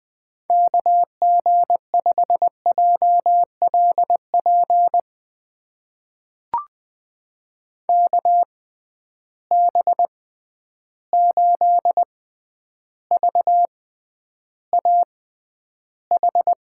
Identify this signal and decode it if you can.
Morse code, heard as a keyed tone, about 20 wpm, spaced as for 10 wpm, 700 Hz. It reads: KG5JLP KB8VAH
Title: USA Call Signs - 6 Characters - 6x Character Spacing - 20wpm